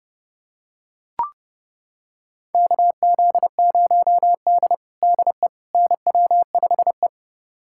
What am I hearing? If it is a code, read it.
Morse code, heard as a keyed tone, about 30 wpm, 700 Hz. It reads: KZ0D DE NW5E
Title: USA Callsign DE Callsign - 4 Characters - 30wpm